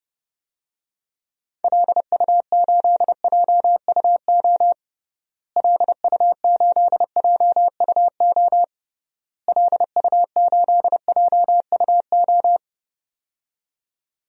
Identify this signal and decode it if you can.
Morse code, heard as a keyed tone, about 32 wpm, 700 Hz. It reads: LU8JUO LU8JUO LU8JUO
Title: DXCC-100 Call Signs - 6 Characters - Repeated 3x - 30wpm